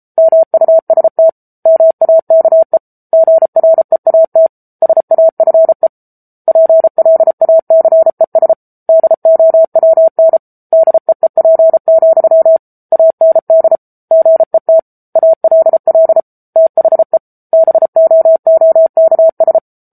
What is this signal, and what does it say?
Morse code, heard as a keyed tone, about 34 wpm, 650 Hz. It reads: MUST MAKE GREAT SAFE PLACES DOWN DEEP, AND GET ALL THE BOOKS
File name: Book Two / 07-Chapter 7 - 33 WPM